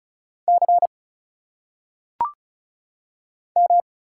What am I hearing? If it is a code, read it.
Morse code, heard as a keyed tone, about 34 wpm, 700 Hz. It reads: C M
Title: Single Letters - Phonetic Pronunciation 35wpm